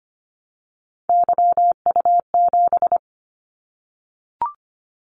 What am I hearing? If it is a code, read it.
Morse code, heard as a keyed tone, about 25 wpm, 700 Hz. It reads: YU7
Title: Callsign Prefixes for Top 25 DXCC - 25wpm